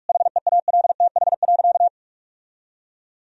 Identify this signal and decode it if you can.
Morse code, heard as a keyed tone, about 43 wpm, 700 Hz. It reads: HEALTH.
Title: Sentences from Top 1000 Words - Review Part 3 - 45wpm